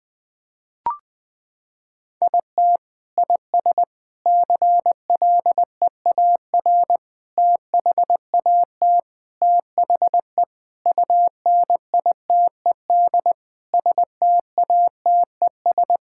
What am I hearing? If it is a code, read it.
Morse code, heard as a keyed tone, about 20 wpm, 700 Hz. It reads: IT IS CLEAR THAT THE UNITED STATES
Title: Sentences with 4-Word Phrases - 20wpm